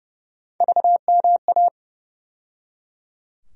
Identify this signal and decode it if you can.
Morse code, heard as a keyed tone, about 31 wpm, 700 Hz. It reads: VMA
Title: USA Call Sign Postfixes - 3 Letters 30wpm